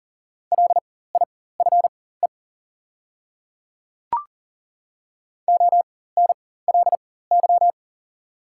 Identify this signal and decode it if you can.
Morse code, heard as a keyed tone, about 40 wpm, spaced as for 19 wpm, 700 Hz. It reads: LIFE ONLY
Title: 4x Character Spacing - Top 300 Words - 40wpm